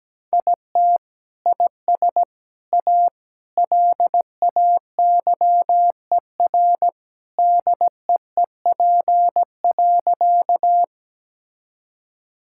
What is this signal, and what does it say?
Morse code, heard as a keyed tone, about 17 wpm, 700 Hz. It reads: IT IS A LAYER DEEP.
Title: Sentences from Top 1600 Words 17wpm